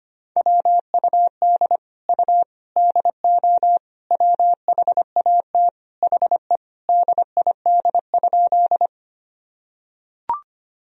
Morse code, heard as a keyed tone, about 25 wpm, 700 Hz. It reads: WUD U DO WHAT HE DID?
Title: Sentences from Top 100 Words and Abbreviations 25wpm